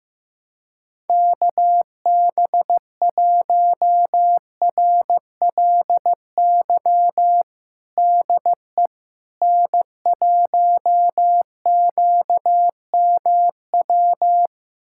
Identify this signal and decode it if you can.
Morse code, heard as a keyed tone, about 16 wpm, 700 Hz. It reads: KB1RLY DE N1QMW
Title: USA Callsign DE Callsign 15wpm